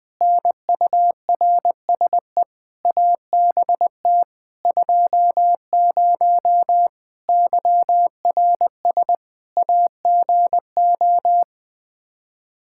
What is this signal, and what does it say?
Morse code, heard as a keyed tone, about 20 wpm, 700 Hz. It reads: NURSE ABT 20 YRS AGO
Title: Practice QSO - Exchange 3 - More Info - 20wpm